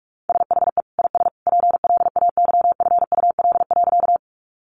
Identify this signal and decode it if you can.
Morse code, heard as a keyed tone, about 44 wpm, 700 Hz. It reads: SHE IS PLAYFUL.
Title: Easy Phrases 45wpm